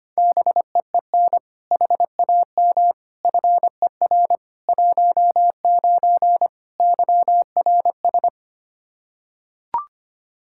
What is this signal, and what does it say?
Morse code, heard as a keyed tone, about 25 wpm, 700 Hz. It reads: BEEN HAM FER 19 YRS
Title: Practice QSO - Exchange 3 - More Info - 25wpm